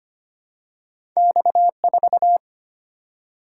Morse code, heard as a keyed tone, about 25 wpm, 700 Hz. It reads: X4
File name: Two Letter-Number Combinations - 25wpm